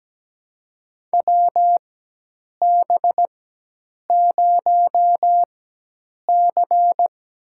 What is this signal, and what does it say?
Morse code, heard as a keyed tone, about 17 wpm, spaced as for 8 wpm, 700 Hz. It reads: WB0C
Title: USA Call Signs - 4 Characters - 4x Character Spacing - 17wpm